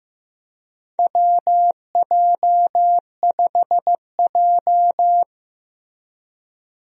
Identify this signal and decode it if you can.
Morse code, heard as a keyed tone, about 15 wpm, 700 Hz. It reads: WJ5J
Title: USA Call Signs - 4 Characters - Repeated 2x - 15wpm